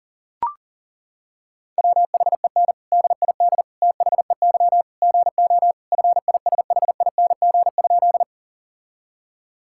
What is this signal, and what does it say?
Morse code, heard as a keyed tone, about 40 wpm, 700 Hz. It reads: WHEN DID THEY GO FISHING?
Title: Sentences from Top 1000 Words - Review Part 5 - 40wpm